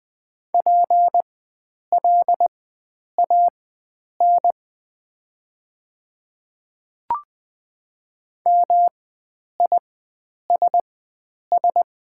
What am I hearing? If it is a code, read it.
Morse code, heard as a keyed tone, about 20 wpm, spaced as for 9 wpm, 690 Hz. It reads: PLAN MISS
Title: 4x Character Spacing - Top 500 Words - 20wpm